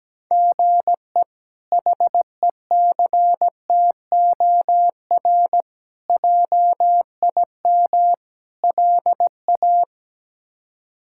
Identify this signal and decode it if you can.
Morse code, heard as a keyed tone, about 17 wpm, 700 Hz. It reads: GE HECTOR JIM LA
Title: K1USN SST QSO Practice - 17wpm